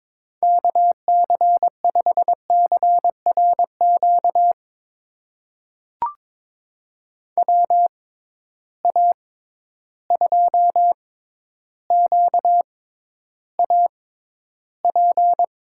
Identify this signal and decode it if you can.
Morse code, heard as a keyed tone, about 22 wpm, spaced as for 11 wpm, 700 Hz. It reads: KC5CRQ WA2QAP
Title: USA Call Signs - 6 Characters - 6x Character Spacing - 22wpm